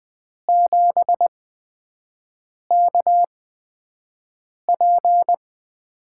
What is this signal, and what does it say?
Morse code, heard as a keyed tone, about 20 wpm, spaced as for 5 wpm, 700 Hz. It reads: 7KP
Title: USA Call Signs - 5 Characters - 8x Character Spacing - 20wpm